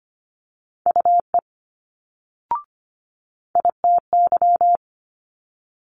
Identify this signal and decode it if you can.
Morse code, heard as a keyed tone, about 24 wpm, 700 Hz. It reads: UE ITY
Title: Multi-letter Phonemes Quick Fire 25wpm